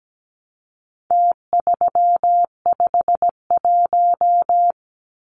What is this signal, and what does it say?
Morse code, heard as a keyed tone, about 17 wpm, 700 Hz. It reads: T351
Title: Numbers - Contest Style - 4-Digit 17wpm